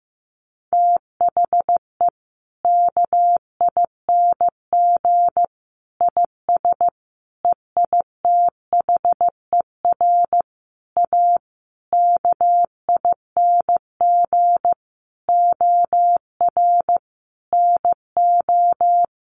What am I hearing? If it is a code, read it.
Morse code, heard as a keyed tone, about 15 wpm, 700 Hz. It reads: THE KING IS EITHER A KING OR NO
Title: Sentences from Top 600 Words 15wpm